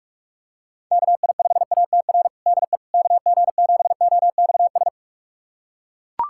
Morse code, heard as a keyed tone, about 41 wpm, 700 Hz. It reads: KI5ATR DE KG7OXS
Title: USA Callsign DE Callsign 45wpm